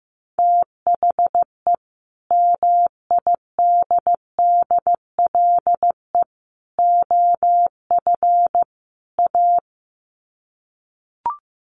Morse code, heard as a keyed tone, about 14 wpm, 700 Hz. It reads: THE MIDDLE OF A
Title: Sets of 4 Words 15wpm